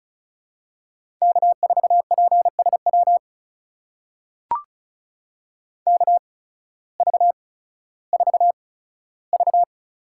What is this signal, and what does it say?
Morse code, heard as a keyed tone, about 36 wpm, spaced as for 16 wpm, 700 Hz. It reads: K4PSW KV4V
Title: USA Call Signs - 5 Characters - 8x Character Spacing - 35wpm